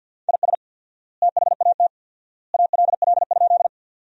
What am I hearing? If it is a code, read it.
Morse code, heard as a keyed tone, about 50 wpm, 700 Hz. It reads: IS THAT ALL?
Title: 4x Word-Spacing - Sentences from Top 100 Words - 50wpm